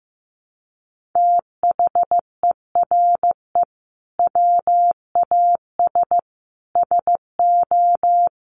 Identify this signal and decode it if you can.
Morse code, heard as a keyed tone, about 15 wpm, 700 Hz. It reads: THERE WAS SO
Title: Sentences from Top 1100 Words 15wpm